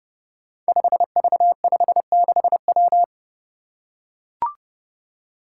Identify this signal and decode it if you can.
Morse code, heard as a keyed tone, about 30 wpm, 700 Hz. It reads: 5V56W
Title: Five Challenging Letter-Number Combinations 30wpm